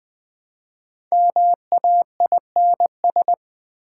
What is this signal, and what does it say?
Morse code, heard as a keyed tone, about 20 wpm, 700 Hz. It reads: MAINS
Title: Build A Word - 5 Letters At A Time - 10-Letter Words - 20wpm